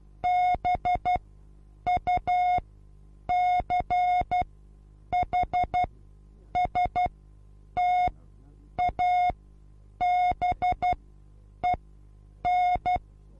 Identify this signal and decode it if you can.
Morse code, heard as a keyed tone, about 12 wpm, spaced as for 8 wpm, 720 Hz. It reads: BUCHSTABEN
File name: Onlinelehrgang / morsen / DJ4UF Morsen23